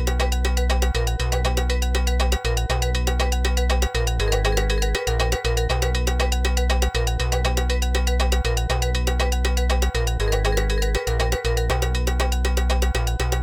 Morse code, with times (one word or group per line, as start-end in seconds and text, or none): none